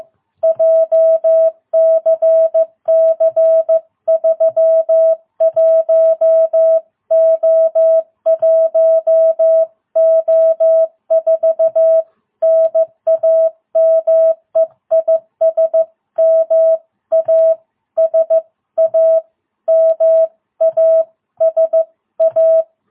Morse code, heard as a keyed tone, about 14 wpm, 660 Hz. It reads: JCC31O1O4 NAME IS MASA MASA
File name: jh8cbh / 2022cw